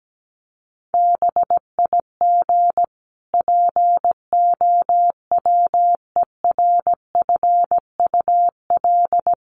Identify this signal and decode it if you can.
Morse code, heard as a keyed tone, about 17 wpm, 700 Hz. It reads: BIG POWERFUL
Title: Sentences from Top 1300 Words 17wpm